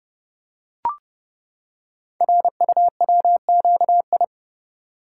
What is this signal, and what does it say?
Morse code, heard as a keyed tone, about 30 wpm, 700 Hz. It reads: RUWQI
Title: Code Groups - 5 Letters - 30wpm